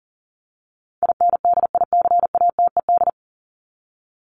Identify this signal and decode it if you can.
Morse code, heard as a keyed tone, about 40 wpm, 700 Hz. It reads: INDICATED
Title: ED Suffix - Top 100 Words with Base Words 40wpm